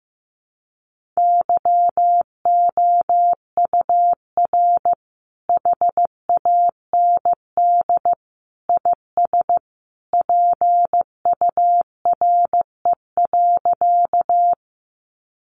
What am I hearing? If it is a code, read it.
Morse code, heard as a keyed tone, about 15 wpm, 700 Hz. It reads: YOUR HAND IS PURE.
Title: Sentences from Top 1500 Words 15wpm